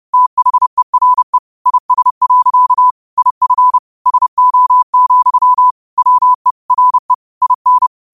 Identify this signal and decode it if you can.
Morse code, heard as a keyed tone, about 30 wpm, 1000 Hz. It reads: THERE IS. IF SO, WE RE IN
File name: qotdfiles / 30wpm